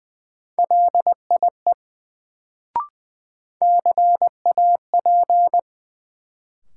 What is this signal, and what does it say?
Morse code, heard as a keyed tone, about 21 wpm, 700 Hz. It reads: LIE CAP
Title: Sentences from Top 1600 Words - New Word Drill - 20wpm